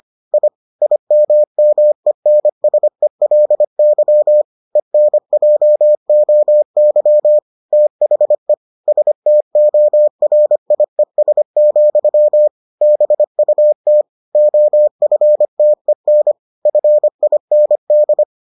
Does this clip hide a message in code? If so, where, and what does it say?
Yes: Morse code from 0.3 to 18.2 s, heard as a keyed tone, about 25 wpm, 600 Hz. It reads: I IMMENSELY ENJOY THE STORIES, BUT OFTEN FIND